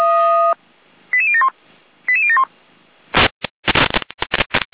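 Five-tone selective call (ZVEI), heard as quick runs of five tones, tones of about 70 ms. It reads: 88071 88071
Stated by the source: ZVEI five-tone selective call received with an SDR.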